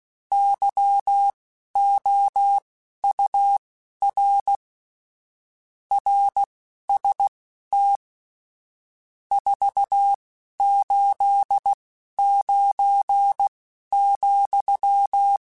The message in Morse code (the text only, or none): YOUR RST 489,